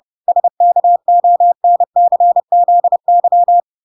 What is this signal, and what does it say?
Morse code, heard as a keyed tone, about 30 wpm, 700 Hz. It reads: SKONCZY